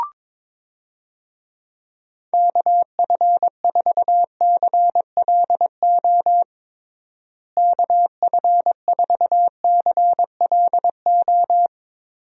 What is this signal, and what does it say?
Morse code, heard as a keyed tone, about 22 wpm, 700 Hz. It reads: KF4CLO KF4CLO
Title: USA Call Signs - 6 Characters - Repeated 3x - 22wpm